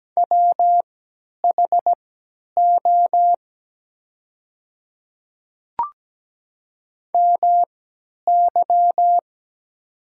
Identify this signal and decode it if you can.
Morse code, heard as a keyed tone, about 17 wpm, spaced as for 10 wpm, 690 Hz. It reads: WHO MY
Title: 3x Character Spacing - Top 100 Words - 17wpm